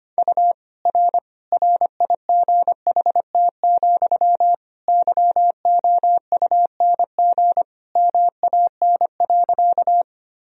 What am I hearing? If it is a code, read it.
Morse code, heard as a keyed tone, about 25 wpm, 700 Hz. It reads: U R RIGHT, YOUNG MAN.
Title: Sentences from Top 400 Words and Abbreviations 25wpm